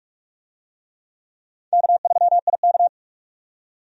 Morse code, heard as a keyed tone, about 46 wpm, 700 Hz. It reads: K3IK
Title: USA Call Signs - 4 Characters - 45wpm